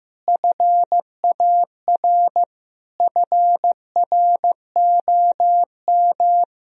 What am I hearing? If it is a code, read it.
Morse code, heard as a keyed tone, about 15 wpm, 700 Hz. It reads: FAR FROM
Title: Sentences from Top 1000 Words - Review Part 1 - 15wpm